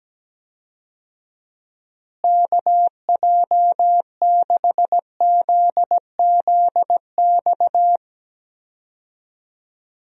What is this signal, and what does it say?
Morse code, heard as a keyed tone, about 17 wpm, 700 Hz. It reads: KJ6ZZX
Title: USA Call Signs - 6 Characters - Repeated 2x - 17wpm